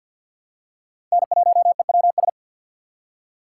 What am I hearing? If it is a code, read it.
Morse code, heard as a keyed tone, about 53 wpm, 700 Hz. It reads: N1EWS
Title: USA Call Signs - 5 Characters - 2x Character Spacing - 50wpm